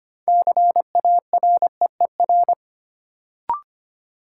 Morse code, heard as a keyed tone, about 25 wpm, 700 Hz. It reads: CAREER
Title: Sentences from Top 1000 Words - New Word Drill - 25wpm